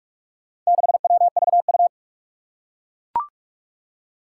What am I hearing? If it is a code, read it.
Morse code, heard as a keyed tone, about 41 wpm, 700 Hz. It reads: 6WVU
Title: Four Challenging Letter-Number Combinations 45wpm